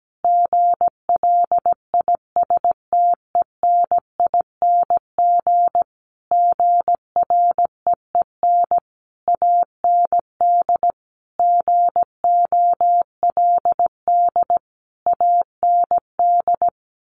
Morse code, heard as a keyed tone, about 17 wpm, 700 Hz. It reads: GLISTENING GREEN AND GOLD AND
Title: A Sound Of Thunder 17wpm